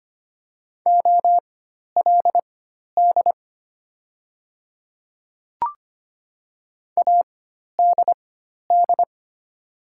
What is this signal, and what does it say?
Morse code, heard as a keyed tone, about 25 wpm, spaced as for 12 wpm, 700 Hz. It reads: OLD ADD
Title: 4x Character Spacing - Top 200 Words - 25wpm